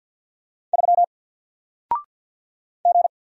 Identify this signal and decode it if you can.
Morse code, heard as a keyed tone, about 48 wpm, 700 Hz. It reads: 3 G
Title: Single Character (Letter:Number:Punctuation:Prosign) 50wpm